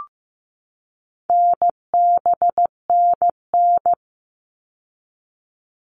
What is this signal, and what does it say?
Morse code, heard as a keyed tone, about 15 wpm, 700 Hz. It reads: NBNN
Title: All Cut Numbers - 4-Digit 15wpm